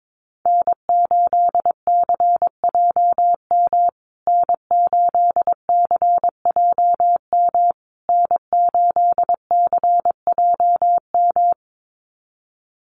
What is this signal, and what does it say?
Morse code, heard as a keyed tone, about 22 wpm, 700 Hz. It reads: N8CJM N8CJM N8CJM
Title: USA Call Signs Repeated 3x 22wpm